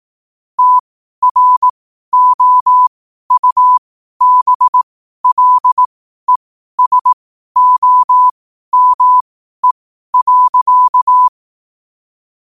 Morse code, heard as a keyed tone, about 18 wpm, spaced as for 12 wpm, 1000 Hz. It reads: TROUBLESOME.
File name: qotdfiles / 12wpm